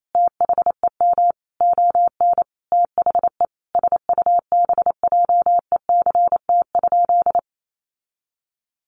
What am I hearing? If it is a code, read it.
Morse code, heard as a keyed tone, about 28 wpm, 700 Hz. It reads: THEM ON THE SUBJECT?
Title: Sentences from Top 800 Words 28wpm